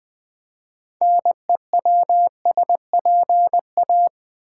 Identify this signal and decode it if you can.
Morse code, heard as a keyed tone, about 20 wpm, 700 Hz. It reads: NEWSPA
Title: Build A Word - 3 Letters At A Time - 9-Letter Words - 20wpm